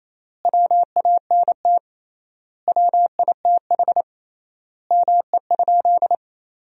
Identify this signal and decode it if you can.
Morse code, heard as a keyed tone, about 28 wpm, 700 Hz. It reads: WANT WITH ME?
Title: 3x Word-Spacing - Sentences from Top 200 Words - 28wpm